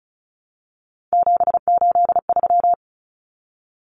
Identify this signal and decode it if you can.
Morse code, heard as a keyed tone, about 34 wpm, 700 Hz. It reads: 783
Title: Common Cut Numbers - 3-Digit 35wpm